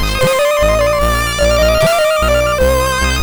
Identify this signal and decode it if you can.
There is no code.